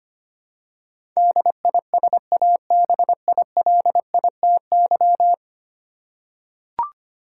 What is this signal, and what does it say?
Morse code, heard as a keyed tone, about 25 wpm, 700 Hz. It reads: DISABILITY
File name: Ten Letter Words - 25wpm